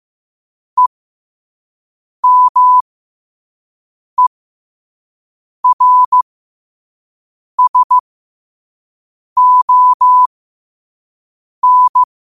Morse code, heard as a keyed tone, about 15 wpm, spaced as for 5 wpm, 1000 Hz. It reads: EMERSON
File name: qotdfiles / 5wpm